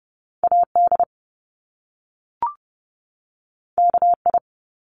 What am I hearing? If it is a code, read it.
Morse code, heard as a keyed tone, about 30 wpm, 700 Hz. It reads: AD KI
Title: Top Two Letter Combinations in Words 30wpm